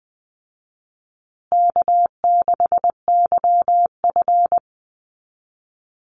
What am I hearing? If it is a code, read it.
Morse code, heard as a keyed tone, about 20 wpm, 700 Hz. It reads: K6YF
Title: Easy Call Signs Repeated 2x - 20wpm